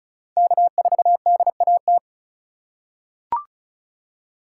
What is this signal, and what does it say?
Morse code, heard as a keyed tone, about 36 wpm, 700 Hz. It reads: K4DAT
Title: USA Call Signs - 3x Character Spacing - 35wpm